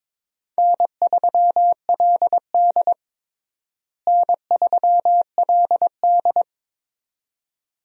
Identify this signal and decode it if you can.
Morse code, heard as a keyed tone, about 22 wpm, 700 Hz. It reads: N3LD N3LD
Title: USA Call Signs - 4 Characters - Repeated 2x - 22wpm